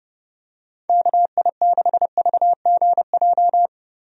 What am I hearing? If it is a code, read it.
Morse code, heard as a keyed tone, about 32 wpm, 700 Hz. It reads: KI6VGJ
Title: USA Call Signs - 6 Characters - Repeated 2x - 30wpm